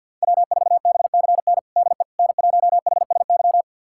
Morse code, heard as a keyed tone, about 53 wpm, 700 Hz. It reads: W4BCN DE N1HSY
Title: USA Callsign DE Callsign - 5 Characters - 50wpm